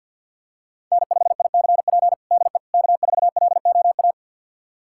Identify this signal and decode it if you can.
Morse code, heard as a keyed tone, about 47 wpm, 700 Hz. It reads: N5ICP DE K4LOA
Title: USA Callsign DE Callsign - 5 Characters - 50wpm